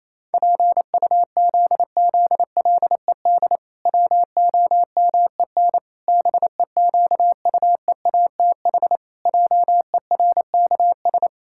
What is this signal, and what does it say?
Morse code, heard as a keyed tone, about 28 wpm, 700 Hz. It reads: PUZZLED WOMEN BEQUEATH JERKS
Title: Pangram Challenge v2 - 28wpm